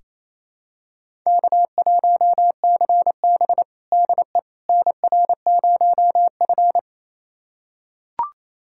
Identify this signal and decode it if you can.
Morse code, heard as a keyed tone, about 27 wpm, 700 Hz. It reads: K1CB DE NR0F
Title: USA Callsign DE Callsign - 4 Characters - 28wpm